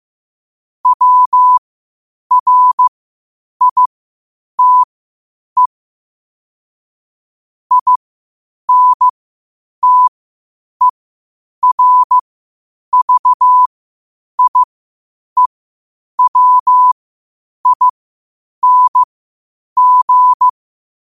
Morse code, heard as a keyed tone, about 15 wpm, spaced as for 8 wpm, 1000 Hz. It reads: WRITE INTERVIEWING
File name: qotdfiles / 8wpm